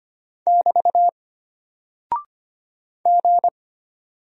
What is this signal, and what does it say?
Morse code, heard as a keyed tone, about 25 wpm, 700 Hz. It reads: <BT> G
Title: Single Character (Letter:Number:Punctuation:Prosign) Alt 25wpm